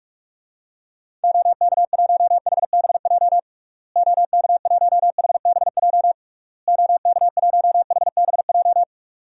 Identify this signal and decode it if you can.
Morse code, heard as a keyed tone, about 46 wpm, 700 Hz. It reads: OK1HBJ OK1HBJ OK1HBJ
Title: DXCC-100 Call Signs - 6 Characters - Repeated 3x - 45wpm